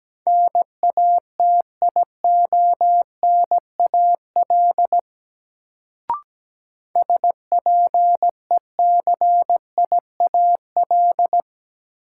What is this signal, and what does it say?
Morse code, heard as a keyed tone, about 17 wpm, 700 Hz. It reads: NATIONAL SPECIAL
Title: Sentences from Top 1200 Words - New Word Drill Alt - 17wpm